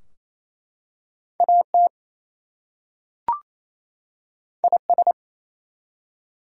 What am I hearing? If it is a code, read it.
Morse code, heard as a keyed tone, about 27 wpm, 700 Hz. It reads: AT IS
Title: Two Letter Words - 28wpm